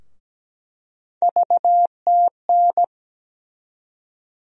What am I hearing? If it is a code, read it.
Morse code, heard as a keyed tone, about 17 wpm, 700 Hz. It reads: VTN
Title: USA Call Sign Postfixes - 3 Letters 17wpm